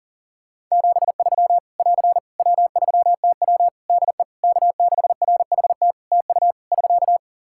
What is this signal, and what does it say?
Morse code, heard as a keyed tone, about 40 wpm, 700 Hz. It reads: 73 <AR> W3TW DE K6RHT TU <SK>
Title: Practice QSO - Exchange 4 - Ending - 40wpm